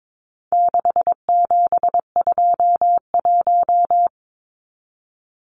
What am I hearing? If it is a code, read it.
Morse code, heard as a keyed tone, about 22 wpm, 700 Hz. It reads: 6721
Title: Numbers - 4-Digit 22wpm